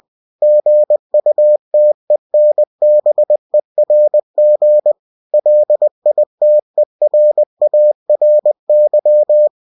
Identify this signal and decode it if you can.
Morse code, heard as a keyed tone, about 20 wpm, 600 Hz. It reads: GUTENBERG LITERARY
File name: cw-books / winnie-the-pooh / winnie-20wpm-page-0120